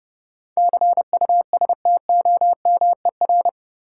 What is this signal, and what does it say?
Morse code, heard as a keyed tone, about 30 wpm, 700 Hz. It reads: CUSTOMER
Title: Sentences from Top 1100 Words - New Word Drill - 30wpm